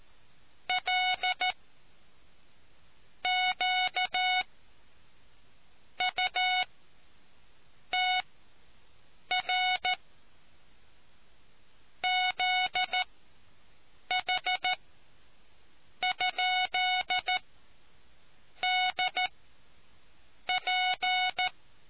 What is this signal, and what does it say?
Morse code, heard as a keyed tone, about 14 wpm, 740 Hz. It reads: L Q U T R Z H ? D P